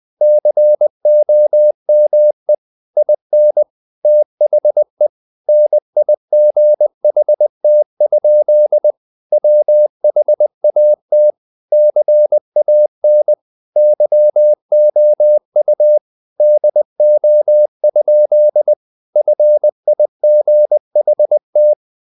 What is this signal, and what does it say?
Morse code, heard as a keyed tone, about 20 wpm, 600 Hz. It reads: COME IN THE NIGHT? WHAT CAN YOU DO? FIGHT
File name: the old man and the sea part 2 20wpm